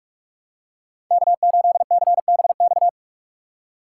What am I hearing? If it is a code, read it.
Morse code, heard as a keyed tone, about 48 wpm, 700 Hz. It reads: K8CBX
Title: USA Call Signs - 5 Characters - 4x Character Spacing - 45wpm